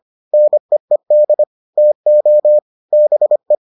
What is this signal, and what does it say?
Morse code, heard as a keyed tone, about 25 wpm, 600 Hz. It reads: NEED TO BE